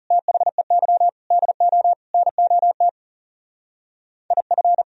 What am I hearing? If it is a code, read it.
Morse code, heard as a keyed tone, about 35 wpm, 700 Hz. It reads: THEY DO NOT IF
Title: Speed-Racing - Sets of 4 Words 30wpm